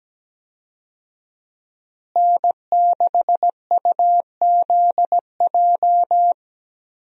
Morse code, heard as a keyed tone, about 17 wpm, 700 Hz. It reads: N6UZJ
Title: USA Call Signs - 5 Characters - Repeated 3x - 17wpm